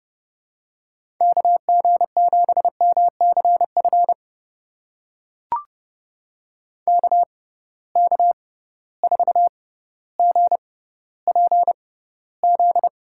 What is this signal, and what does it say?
Morse code, heard as a keyed tone, about 30 wpm, spaced as for 15 wpm, 700 Hz. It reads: KG7MCF KK4GPZ
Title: USA Call Signs - 6 Characters - 6x Character Spacing - 30wpm